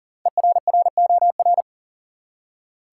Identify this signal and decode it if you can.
Morse code, heard as a keyed tone, about 37 wpm, 700 Hz. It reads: ERROR
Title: Five Letter Words - 40wpm